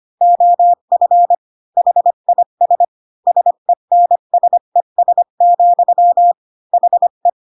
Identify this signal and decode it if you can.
Morse code, heard as a keyed tone, about 25 wpm, 700 Hz. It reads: OF HIS SENSES, HE
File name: Pinocchio / Pinocchio ch1 20 25 ik1hge